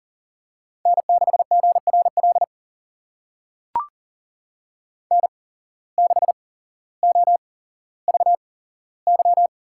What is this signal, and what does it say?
Morse code, heard as a keyed tone, about 40 wpm, spaced as for 17 wpm, 700 Hz. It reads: N6GRL N6OVY
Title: USA Call Signs - 8x Character Spacing - 40wpm